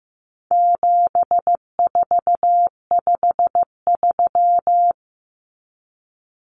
Morse code, heard as a keyed tone, about 15 wpm, 700 Hz. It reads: 7453
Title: Numbers - 4-Digit 15wpm